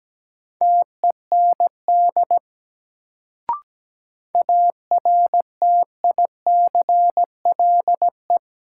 Morse code, heard as a keyed tone, about 17 wpm, 700 Hz. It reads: TEND ARTICLE
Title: Sentences from Top 1000 Words - Review Part 5 - New Word Drill - 17wpm